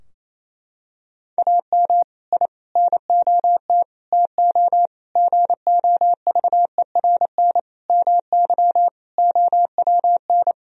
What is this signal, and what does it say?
Morse code, heard as a keyed tone, about 28 wpm, 700 Hz. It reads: AM I NOT TO GOVERN MY OWN
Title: Sentences from Top 1000 Words - Review Part 2 - 28wpm